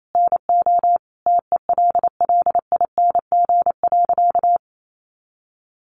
Morse code, heard as a keyed tone, about 28 wpm, 700 Hz. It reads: NO TELLING.
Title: A Sound Of Thunder 28wpm